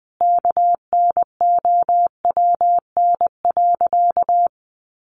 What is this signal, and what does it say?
Morse code, heard as a keyed tone, about 20 wpm, 700 Hz. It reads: KNOWN.
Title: Sentences from Top 1300 Words 20wpm